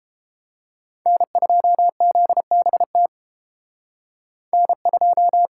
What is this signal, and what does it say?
Morse code, heard as a keyed tone, about 34 wpm, 700 Hz. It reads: N2ZBT N2